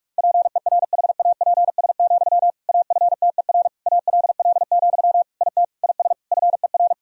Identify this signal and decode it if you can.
Morse code, heard as a keyed tone, about 46 wpm, 690 Hz. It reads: PERHAPS, AFTER ALL, IT IS FER
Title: Sentences from Top 500 Words and Abbreviations 45wpm